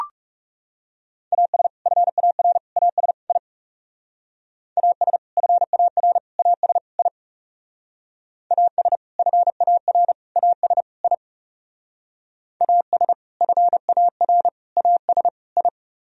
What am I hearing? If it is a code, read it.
Morse code, heard as a keyed tone, about 35 wpm, 700 Hz. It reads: AS FAR AS I AS FAR AS I AS FAR AS I AS FAR AS I
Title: Speed-Racing - Sets of 4 Words 30wpm